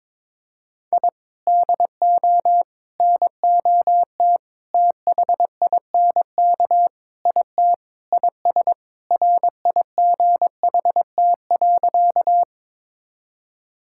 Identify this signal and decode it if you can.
Morse code, heard as a keyed tone, about 22 wpm, 700 Hz. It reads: I DO NOT THINK IT IS RIGHT.
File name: practice / 22wpm / Sentences with 4-Word Phrases - 22wpm